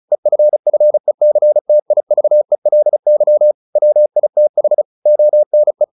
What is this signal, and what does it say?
Morse code, heard as a keyed tone, about 34 wpm, 600 Hz. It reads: EFFECTIVELY WITH ONE
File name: the old man and the sea 2 35wpm